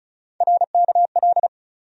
Morse code, heard as a keyed tone, about 34 wpm, 700 Hz. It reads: RKL
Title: USA Call Sign Postfixes - 3 Letters 35wpm